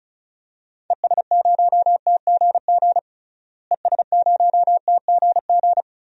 Morse code, heard as a keyed tone, about 36 wpm, 700 Hz. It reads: ES0TGG ES0TGG
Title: DXCC-100 Call Signs - 6 Characters - Repeated 3x - 35wpm